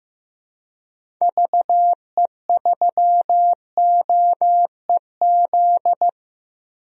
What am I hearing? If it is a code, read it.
Morse code, heard as a keyed tone, about 16 wpm, 700 Hz. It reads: VE3OEZ
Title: DXCC-100 Call Signs - 6 Characters - 15wpm